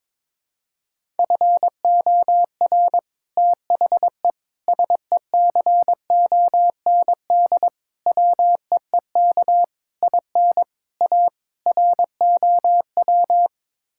Morse code, heard as a keyed tone, about 21 wpm, 700 Hz. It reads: FOR THE SECOND WEEK IN A ROW
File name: practice / 22wpm / Sets of 7 Words - 22wpm